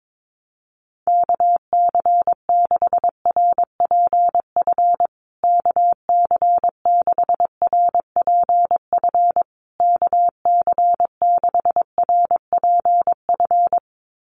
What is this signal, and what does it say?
Morse code, heard as a keyed tone, about 22 wpm, 700 Hz. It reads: KC6RPF KC6RPF KC6RPF
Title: USA Call Signs Repeated 3x 22wpm